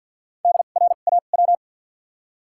Morse code, heard as a keyed tone, about 53 wpm, 700 Hz. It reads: D R A W
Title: 2x Character Spacing - Top 300 Words - 50wpm